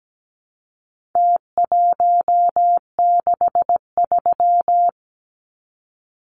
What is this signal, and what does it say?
Morse code, heard as a keyed tone, about 17 wpm, 700 Hz. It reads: T163
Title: Numbers - Contest Style - 4-Digit 17wpm